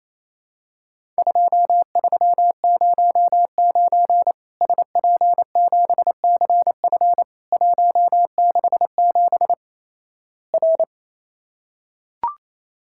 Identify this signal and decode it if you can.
Morse code, heard as a keyed tone, about 28 wpm, 700 Hz. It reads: 2309 SP7CF 167 R
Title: Worked All Europe DX Contest - QTC Exchange - 28wpm